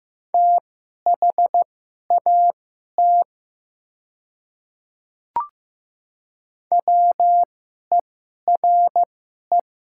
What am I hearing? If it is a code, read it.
Morse code, heard as a keyed tone, about 15 wpm, spaced as for 11 wpm, 700 Hz. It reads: THAT WERE
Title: 2x Character Spacing - Top 100 Words - 15wpm